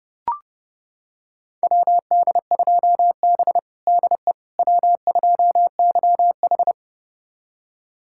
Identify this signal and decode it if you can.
Morse code, heard as a keyed tone, about 30 wpm, 700 Hz. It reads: WD2B DE W2YH
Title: USA Callsign DE Callsign - 4 Characters - 30wpm